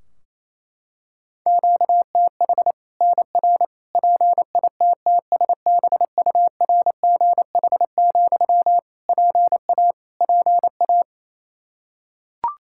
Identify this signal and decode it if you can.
Morse code, heard as a keyed tone, about 28 wpm, 700 Hz. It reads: QTH NR PITTSBURGH, PA PA
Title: Practice QSO - Exchange 1 - Beginning - 28wpm